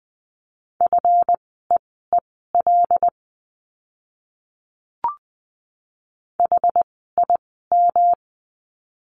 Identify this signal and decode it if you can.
Morse code, heard as a keyed tone, about 20 wpm, spaced as for 14 wpm, 700 Hz. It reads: FEEL HIM
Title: 237 Lesson 38f - Letter q - Words 20wpm